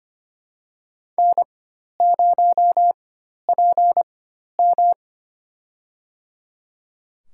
Morse code, heard as a keyed tone, about 25 wpm, spaced as for 12 wpm, 700 Hz. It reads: N0PM
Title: USA Call Signs - 4 Characters - 4x Character Spacing - 25wpm